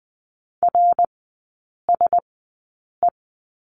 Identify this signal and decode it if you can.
Morse code, heard as a keyed tone, about 20 wpm, spaced as for 8 wpm, 700 Hz. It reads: RSE
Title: 053 Lesson 10e - Letter r - Three Characters 20wpm